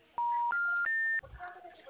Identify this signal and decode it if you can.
Special information tone, with speech.